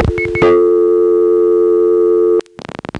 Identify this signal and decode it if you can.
Dial tone (North America), 350 + 440 Hz.
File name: phonetrips / SmyrnaGA3HQ